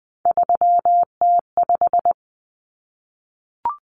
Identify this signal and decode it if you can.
Morse code, heard as a keyed tone, about 20 wpm, 700 Hz. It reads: 3T5
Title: Common Cut Numbers - 3-Digit 20wpm